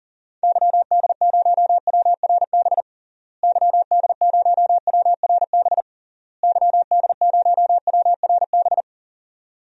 Morse code, heard as a keyed tone, about 37 wpm, 700 Hz. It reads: YD0WRB YD0WRB YD0WRB
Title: DXCC-100 Call Signs - 6 Characters - Repeated 3x - 40wpm